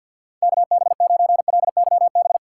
Morse code, heard as a keyed tone, about 53 wpm, 700 Hz. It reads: KB9LYB
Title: USA Call Signs - 6 Characters - 4x Character Spacing - 50wpm